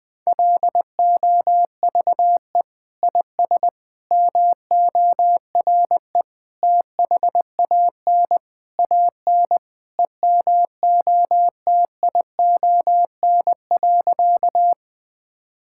Morse code, heard as a keyed tone, about 20 wpm, 700 Hz. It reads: LOVE IS MORE THAN AN EMOTION.
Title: Sentences from Top 1600 Words 20wpm